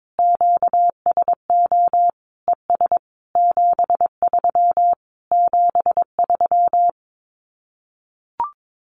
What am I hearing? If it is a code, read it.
Morse code, heard as a keyed tone, about 22 wpm, 700 Hz. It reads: QSO ES 73 73
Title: Common QSO Phrases Encore 22wpm